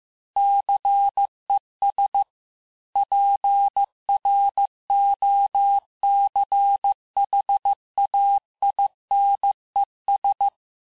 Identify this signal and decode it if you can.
Morse code, heard as a keyed tone, about 15 wpm, 800 Hz. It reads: CES PROCHAINES
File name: cours-du-lundi files / cours-1585-texte-FIN-de-QSO-vitesse-14mpm